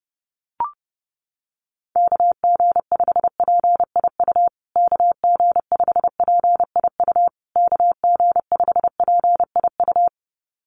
Morse code, heard as a keyed tone, about 30 wpm, 700 Hz. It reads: KG5PIU KG5PIU KG5PIU
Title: USA Call Signs Repeated 3x 30wpm